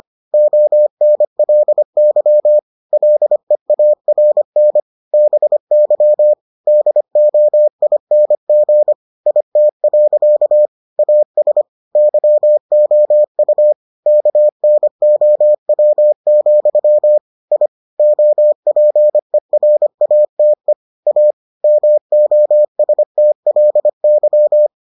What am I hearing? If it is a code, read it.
Morse code, heard as a keyed tone, about 25 wpm, 600 Hz. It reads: ONLY LEARN BY DOING IT. AS YOU KNOW, I OPERATE A MOSTLY